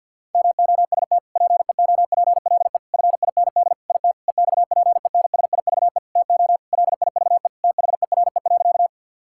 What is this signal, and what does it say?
Morse code, heard as a keyed tone, about 50 wpm, 700 Hz. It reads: MOST PEOPLE FIND IT EXPENSIVE TO LIVE THERE.
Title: Sentences from Top 1000 Words - Review Part 6 - 50wpm